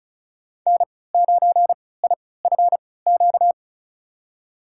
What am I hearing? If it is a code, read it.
Morse code, heard as a keyed tone, about 34 wpm, spaced as for 20 wpm, 700 Hz. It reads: N9IFQ
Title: USA Call Signs - 5 Characters - 3x Character Spacing - 35wpm